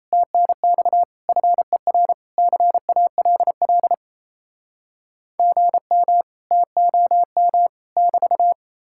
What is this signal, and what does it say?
Morse code, heard as a keyed tone, about 38 wpm, 700 Hz. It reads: TNX FER CALL GM TOM <BT>